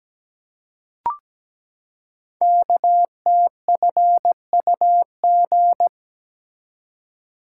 Morse code, heard as a keyed tone, about 17 wpm, 700 Hz. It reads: KTFUG
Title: Code Groups - 5 Letters - 17wpm